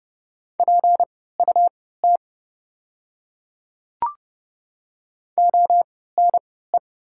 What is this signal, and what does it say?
Morse code, heard as a keyed tone, about 30 wpm, spaced as for 17 wpm, 690 Hz. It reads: PUT ONE
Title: 3x Character Spacing - Top 200 Words - 30wpm